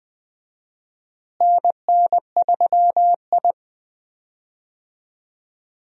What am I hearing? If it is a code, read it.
Morse code, heard as a keyed tone, about 20 wpm, 700 Hz. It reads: NN3I
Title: USA Call Signs - 4 Characters - Repeated 3x - 20wpm